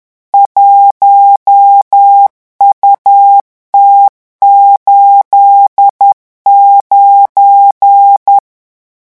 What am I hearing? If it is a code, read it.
Morse code, heard as a keyed tone, about 10 wpm, 800 Hz. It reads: 1UT89
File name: morse mp3 / revision16-vitesse 10 mots